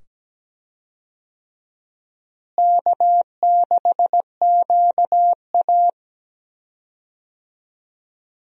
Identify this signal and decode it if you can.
Morse code, heard as a keyed tone, about 17 wpm, 700 Hz. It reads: K6QA
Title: USA Call Signs - 4 Characters - Repeated 2x - 17wpm